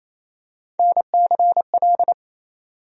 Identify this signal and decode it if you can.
Morse code, heard as a keyed tone, about 28 wpm, 700 Hz. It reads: NCL
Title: USA Call Sign Postfixes - 3 Letters 28wpm